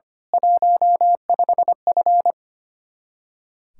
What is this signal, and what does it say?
Morse code, heard as a keyed tone, about 25 wpm, 700 Hz. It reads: 15F
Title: Speed-Racing Common QSO Phrases - 25wpm